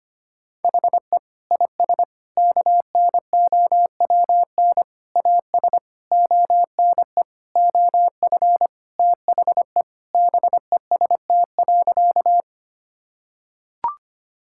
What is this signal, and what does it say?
Morse code, heard as a keyed tone, about 25 wpm, 700 Hz. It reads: HE IS KNOWN AS ONE OF THE BEST.
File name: practice / 25wpm / Sentences with 4-Word Phrases - 25wpm